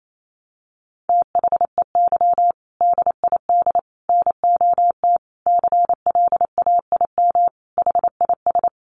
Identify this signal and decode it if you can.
Morse code, heard as a keyed tone, about 28 wpm, 700 Hz. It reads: THEY DID NOT CLAIM HIS
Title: Sentences from Top 1500 Words 28wpm